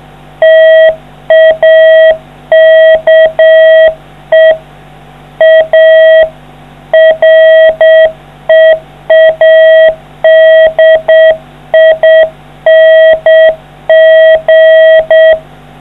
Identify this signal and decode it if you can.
Morse code, heard as a keyed tone, about 7 wpm, 640 Hz. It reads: TAKE A READING